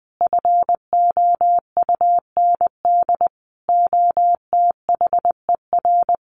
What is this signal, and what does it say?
Morse code, heard as a keyed tone, about 19 wpm, 700 Hz. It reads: FOUND OTHER
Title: Sentences from Top 1500 Words 20wpm